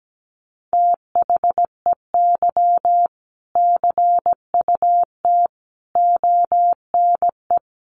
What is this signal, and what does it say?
Morse code, heard as a keyed tone, about 17 wpm, 700 Hz. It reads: THEY CUT ONE
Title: Sentences from Top 1400 Words 17wpm